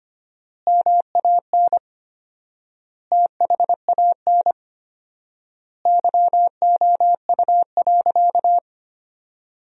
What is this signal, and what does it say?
Morse code, heard as a keyed tone, about 25 wpm, 700 Hz. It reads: MAN THAN YOU.
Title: 4x Word-Spacing - Sentences from Top 400 Words - 25wpm